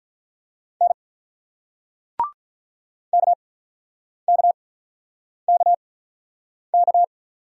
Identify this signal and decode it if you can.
Morse code, heard as a keyed tone, about 42 wpm, spaced as for 9 wpm, 700 Hz. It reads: N KKKK
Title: Speed-Racing Single Letters 35wpm